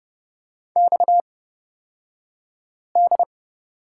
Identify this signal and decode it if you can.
Morse code, heard as a keyed tone, about 30 wpm, spaced as for 5 wpm, 700 Hz. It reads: XD
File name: Lesson 37d - Letter x - Two Characters 30wpm